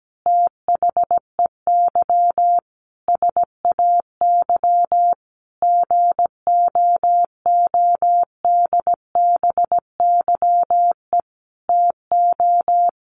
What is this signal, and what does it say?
Morse code, heard as a keyed tone, about 17 wpm, 700 Hz. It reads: THEY SAY GOODBYE TO
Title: Additional Easy Phrases 17wpm